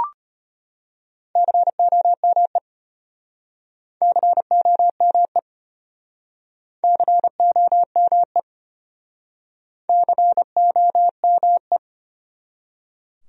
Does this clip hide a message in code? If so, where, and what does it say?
Yes: Morse code from 1.4 to 11.8 s, heard as a keyed tone, about 31 wpm, 700 Hz. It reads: COME COME COME COME